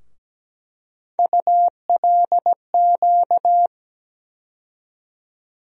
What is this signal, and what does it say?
Morse code, heard as a keyed tone, about 17 wpm, 700 Hz. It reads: ULQ
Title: USA Call Sign Postfixes - 3 Letters 17wpm